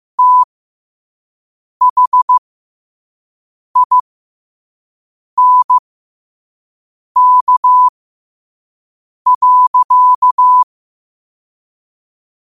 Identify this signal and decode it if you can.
Morse code, heard as a keyed tone, about 15 wpm, spaced as for 5 wpm, 1000 Hz. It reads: THINK.
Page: Quotes for Tue, 12 Aug 2025 in Morse Code at 5 words per minute.